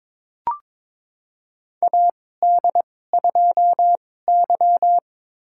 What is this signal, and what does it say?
Morse code, heard as a keyed tone, about 22 wpm, spaced as for 16 wpm, 700 Hz. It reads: AD2Y
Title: USA Call Signs - 4 Characters - 2x Character Spacing - 22wpm